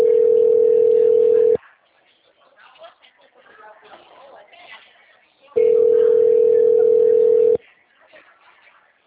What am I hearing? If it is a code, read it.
Ringback tone (North America), 440 + 480 Hz, with speech.
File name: out- -2016- - - .28